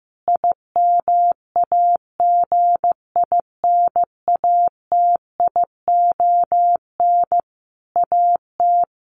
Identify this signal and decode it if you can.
Morse code, heard as a keyed tone, about 14 wpm, 700 Hz. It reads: IMAGINATION AT
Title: Sentences from Top 1300 Words 15wpm